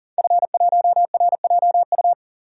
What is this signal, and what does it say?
Morse code, heard as a keyed tone, about 40 wpm, 700 Hz. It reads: F1RJU
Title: DXCC-100 Call Signs - 5 Characters - 40wpm